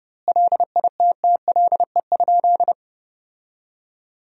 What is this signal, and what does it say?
Morse code, heard as a keyed tone, about 32 wpm, 700 Hz. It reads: LITTLE?
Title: 2x Word-Spacing - Sentences from Top 500 Words - 30wpm